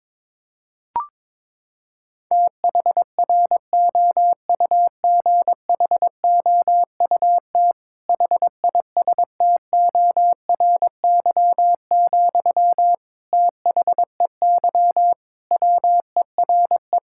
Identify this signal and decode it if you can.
Morse code, heard as a keyed tone, about 21 wpm, 700 Hz. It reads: THROUGHOUT HISTORY, THEY WERE
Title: Sentences from Top 1000 Words - Review Part 5 - 22wpm